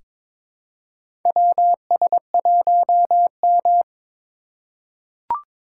Morse code, heard as a keyed tone, about 22 wpm, 700 Hz. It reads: WS1M
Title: USA Call Signs - 4 Characters - 3x Character Spacing - 22wpm